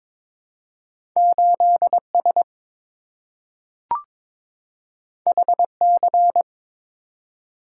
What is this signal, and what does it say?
Morse code, heard as a keyed tone, about 22 wpm, 700 Hz. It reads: 8S HC
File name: practice / 22wpm / Two Letter-Number Combinations - 22wpm